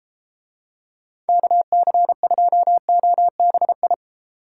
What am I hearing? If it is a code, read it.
Morse code, heard as a keyed tone, about 31 wpm, 700 Hz. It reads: KC2OBI